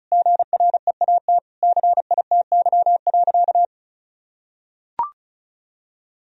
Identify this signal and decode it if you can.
Morse code, heard as a keyed tone, about 36 wpm, 700 Hz. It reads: GREAT CITY.
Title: Sentences from Top 1000 Words - Review Part 1 - 35wpm